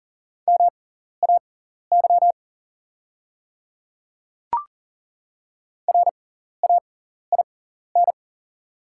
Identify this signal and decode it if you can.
Morse code, heard as a keyed tone, about 40 wpm, spaced as for 14 wpm, 690 Hz. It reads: MAY RAIN
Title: 6x Character Spacing - Top 300 Words - 40wpm